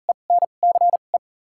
Morse code, heard as a keyed tone, about 37 wpm, spaced as for 28 wpm, 700 Hz. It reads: ENCE
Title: 2x Character Spacing - Top 500 Words - 40wpm